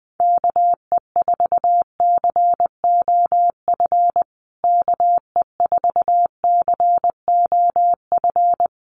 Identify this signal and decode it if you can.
Morse code, heard as a keyed tone, about 20 wpm, 700 Hz. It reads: KE4COF KE4COF
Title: USA Call Signs Repeated 3x 20wpm